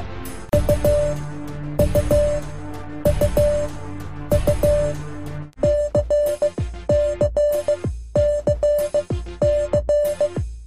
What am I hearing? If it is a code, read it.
Morse code, heard as a keyed tone, about 16 wpm, spaced as for 12 wpm, 590 Hz. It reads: U U U U CCCC
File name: cwa / MorseSongSequence2-H-D-L-U-C